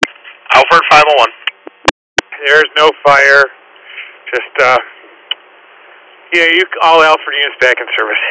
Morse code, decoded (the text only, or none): none